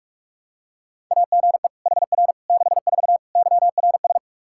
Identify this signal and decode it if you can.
Morse code, heard as a keyed tone, about 46 wpm, 700 Hz. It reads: AGE HR 64 YRS
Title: K1USN SST QSO Practice 45wpm